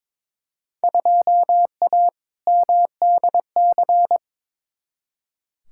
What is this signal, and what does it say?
Morse code, heard as a keyed tone, about 22 wpm, 700 Hz. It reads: 2A MDC
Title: Field Day Exchanges - 22wpm